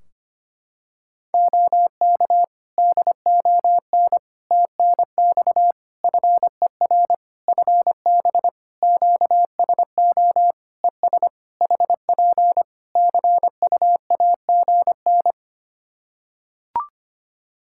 Morse code, heard as a keyed tone, about 25 wpm, 700 Hz. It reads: OK DON TNX FER FB QSO ES HP CUAGN
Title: Practice QSO - Exchange 4 - Ending - 25wpm